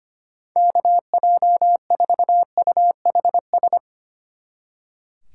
Morse code, heard as a keyed tone, about 31 wpm, 700 Hz. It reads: KJ4UHS
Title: Speed Racing - USA 2x3 Call Signs 25wpm